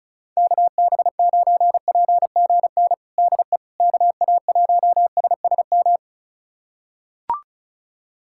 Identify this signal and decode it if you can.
Morse code, heard as a keyed tone, about 34 wpm, 700 Hz. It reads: KB9PGN DE KA1SSM
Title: USA Callsign DE Callsign - 6 Characters - 35wpm